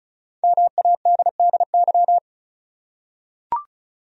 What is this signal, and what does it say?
Morse code, heard as a keyed tone, about 36 wpm, 700 Hz. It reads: MADDY
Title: USA Common Names - Max 5 Letters - 35wpm